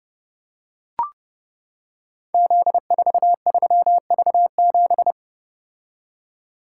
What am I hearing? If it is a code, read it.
Morse code, heard as a keyed tone, about 30 wpm, 700 Hz. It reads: Z43V7
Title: Code Groups - 5 Letter:Numbers - 30wpm